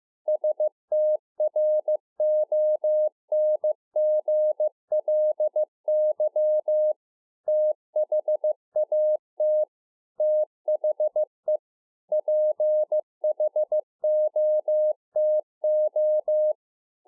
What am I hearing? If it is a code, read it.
Morse code, heard as a keyed tone, about 15 wpm, 610 Hz. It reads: STRONGLY THAT THE PHOTO
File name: 2011-04-mp3 / THE 3.11 6 15WPM